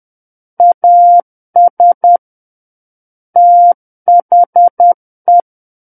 Morse code, heard as a keyed tone, about 10 wpm, 700 Hz. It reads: AS THE
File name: Book One / 01-Chapter 1 - 10 WPM